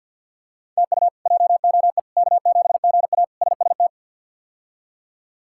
Morse code, heard as a keyed tone, about 51 wpm, 700 Hz. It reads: TU JOE K7GA SST
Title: K1USN SST QSO Practice - 50wpm